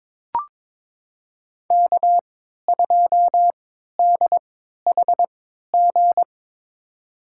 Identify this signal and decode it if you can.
Morse code, heard as a keyed tone, about 22 wpm, spaced as for 13 wpm, 700 Hz. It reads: K2DHG
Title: USA Call Signs - 5 Characters - 3x Character Spacing - 22wpm